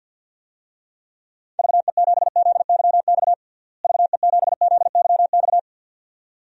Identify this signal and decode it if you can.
Morse code, heard as a keyed tone, about 48 wpm, 700 Hz. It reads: VE7ZYX VE7ZYX
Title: DXCC-100 Call Signs - 6 Characters - Repeated 2x - 50wpm